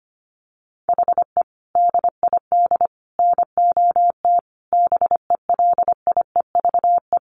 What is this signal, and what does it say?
Morse code, heard as a keyed tone, about 25 wpm, 700 Hz. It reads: HE DID NOT BELIEVE
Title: Sentences from Top 1300 Words 25wpm